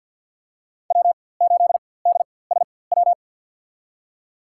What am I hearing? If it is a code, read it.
Morse code, heard as a keyed tone, about 51 wpm, spaced as for 23 wpm, 700 Hz. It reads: W8DSW
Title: USA Call Signs - 5 Characters - 4x Character Spacing - 50wpm